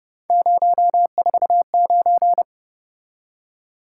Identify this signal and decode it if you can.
Morse code, heard as a keyed tone, about 29 wpm, 700 Hz. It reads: 049
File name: Speed-Racing - Three Letter-Number Combinations - 25wpm